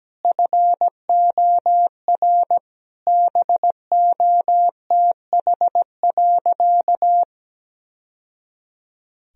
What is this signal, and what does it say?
Morse code, heard as a keyed tone, about 17 wpm, 700 Hz. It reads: FOR BOTH.
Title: Sentences from Top 1000 Words - Review Part 3 - 17wpm